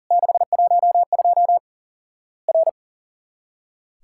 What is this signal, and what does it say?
Morse code, heard as a keyed tone, about 42 wpm, 700 Hz. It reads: 612 R
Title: Worked All Europe DX Contest - QTC Exchange - 40wpm